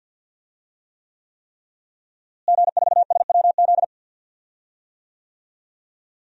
Morse code, heard as a keyed tone, about 52 wpm, 700 Hz. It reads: G4SWZ
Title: DXCC-100 Call Signs - 5 Characters - Repeated 3x - 50wpm